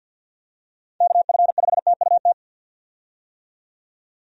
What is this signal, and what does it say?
Morse code, heard as a keyed tone, about 50 wpm, 700 Hz. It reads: KF5TUT
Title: USA Call Signs - 6 Characters - 50wpm